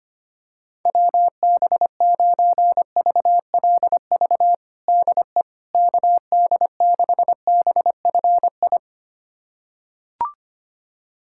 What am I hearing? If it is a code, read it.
Morse code, heard as a keyed tone, about 25 wpm, 700 Hz. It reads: WB9VLV DE KD6BFI
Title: USA Callsign DE Callsign 25wpm